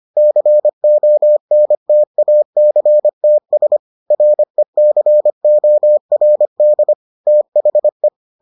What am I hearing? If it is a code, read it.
Morse code, heard as a keyed tone, about 25 wpm, 600 Hz. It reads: CONTACTS RECORD THE